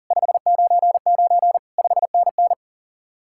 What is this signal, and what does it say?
Morse code, heard as a keyed tone, about 50 wpm, 700 Hz. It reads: 599 5NN